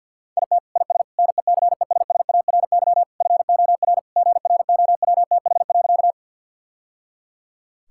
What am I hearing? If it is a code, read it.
Morse code, heard as a keyed tone, about 50 wpm, 700 Hz. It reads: IT IS NECESSARY FOR GROWTH.
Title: Sentences from Top 1000 Words - Review Part 5 - 50wpm